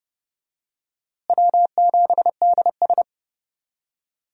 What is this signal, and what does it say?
Morse code, heard as a keyed tone, about 30 wpm, 700 Hz. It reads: W7DS
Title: USA Call Signs - 4 Characters - Repeated 3x - 30wpm